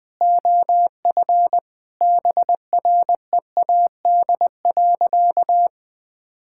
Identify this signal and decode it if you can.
Morse code, heard as a keyed tone, about 20 wpm, 700 Hz. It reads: OF BREAD.
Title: Sentences from Top 1600 Words 20wpm